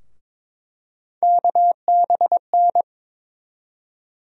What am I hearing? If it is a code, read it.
Morse code, heard as a keyed tone, about 22 wpm, 700 Hz. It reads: KBN
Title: USA Call Sign Postfixes - 3 Letters 22wpm